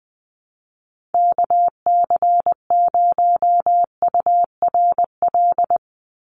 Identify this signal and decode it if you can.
Morse code, heard as a keyed tone, about 20 wpm, 700 Hz. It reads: KC0URL
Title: Call Signs - 20wpm